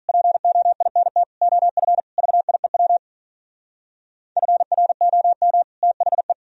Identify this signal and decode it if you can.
Morse code, heard as a keyed tone, about 42 wpm, 700 Hz. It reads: POINT OF VIEW FROM THE
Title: Speed-Racing - Sets of 5 Words 35wpm